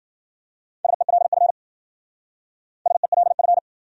Morse code, heard as a keyed tone, about 53 wpm, 700 Hz. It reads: SELF SELF